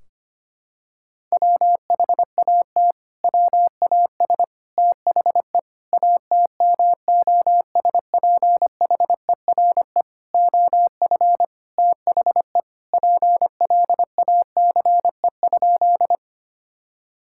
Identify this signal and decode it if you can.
Morse code, heard as a keyed tone, about 25 wpm, 700 Hz. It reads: WHAT WAS THE ATMOSPHERE OF THE PLACE?
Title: Sentences from Top 1600 Words 25wpm